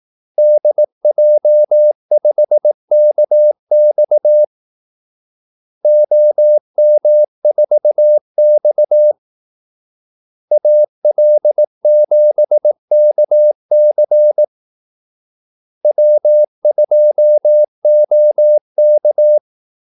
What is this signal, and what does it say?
Morse code, heard as a keyed tone, about 18 wpm, 600 Hz. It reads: DJ5KX OM4X AL7KC W2OK
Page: Kaldesignaler 18 wpm